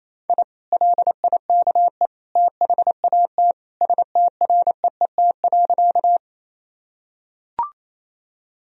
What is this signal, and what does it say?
Morse code, heard as a keyed tone, about 28 wpm, 700 Hz. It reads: I LIKE THAT STREET.
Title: Sentences from Top 1000 Words - Review Part 3 - 28wpm